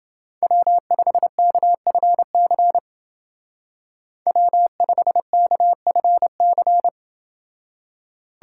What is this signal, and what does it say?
Morse code, heard as a keyed tone, about 28 wpm, 700 Hz. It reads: W5KFC W5KFC